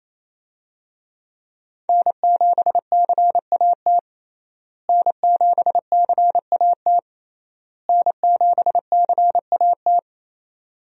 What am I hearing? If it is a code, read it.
Morse code, heard as a keyed tone, about 28 wpm, 700 Hz. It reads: N7CAT N7CAT N7CAT
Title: USA Call Signs - 5 Characters - Repeated 3x - 28wpm